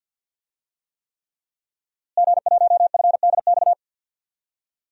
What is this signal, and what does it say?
Morse code, heard as a keyed tone, about 52 wpm, 700 Hz. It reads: G1FDX
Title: DXCC-100 Call Signs - 5 Characters - Repeated 3x - 50wpm